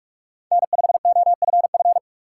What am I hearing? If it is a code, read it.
Morse code, heard as a keyed tone, about 46 wpm, 700 Hz. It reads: N5OFF
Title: USA Call Signs - 5 Characters - 3x Character Spacing - 45wpm